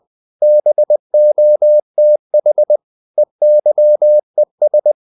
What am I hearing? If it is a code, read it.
Morse code, heard as a keyed tone, about 20 wpm, 600 Hz. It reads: BOTH EYES
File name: the old man and the sea / the old man and the sea part 1 20wpm